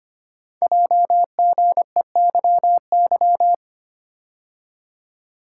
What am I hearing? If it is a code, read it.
Morse code, heard as a keyed tone, about 25 wpm, 700 Hz. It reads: JGEYY
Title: Code Groups - 5 Letters - 25wpm